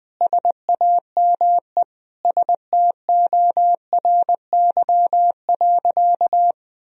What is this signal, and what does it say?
Morse code, heard as a keyed tone, about 20 wpm, 700 Hz. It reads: SAME STORY.
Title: Sentences from Top 1000 Words - Review Part 1 - 20wpm